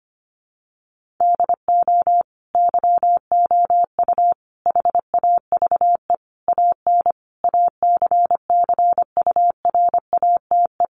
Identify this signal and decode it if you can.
Morse code, heard as a keyed tone, about 25 wpm, 700 Hz. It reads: DO YOU HAVE AN ACCURATE
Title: Sentences from Top 1300 Words 25wpm